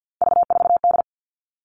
Morse code, heard as a keyed tone, about 50 wpm, 700 Hz. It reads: V4D
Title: All Cut Numbers - 3-Digit 50wpm